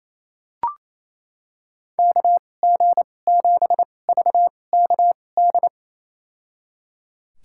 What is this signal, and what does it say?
Morse code, heard as a keyed tone, about 27 wpm, spaced as for 20 wpm, 700 Hz. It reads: KG7VKD
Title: USA Call Signs - 2x Character Spacing - 28wpm